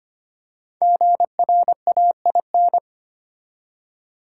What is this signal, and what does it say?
Morse code, heard as a keyed tone, about 25 wpm, 700 Hz. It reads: GRAIN
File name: 5 Letter Words - First 3 Letters Given - 25wpm